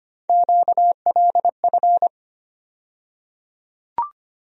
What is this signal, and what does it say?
Morse code, heard as a keyed tone, about 25 wpm, 700 Hz. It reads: QLF
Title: USA Call Sign Postfixes - 3 Letters 25wpm